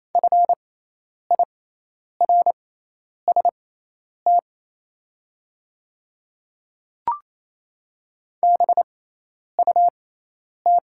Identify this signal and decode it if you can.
Morse code, heard as a keyed tone, about 28 wpm, spaced as for 10 wpm, 700 Hz. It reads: FIRST BUT
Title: 6x Character Spacing - Top 100 Words - 28wpm